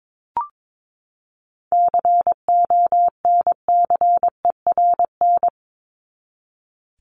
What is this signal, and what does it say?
Morse code, heard as a keyed tone, about 22 wpm, 700 Hz. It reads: CONCERN
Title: ING Suffix - Top 100 Words with Base Words 22wpm